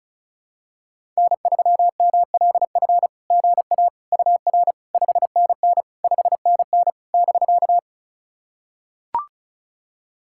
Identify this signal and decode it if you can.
Morse code, heard as a keyed tone, about 34 wpm, 690 Hz. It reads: N3MLF GA UR 5NN 5NN <BK>
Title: SOTA QSO Practice 35wpm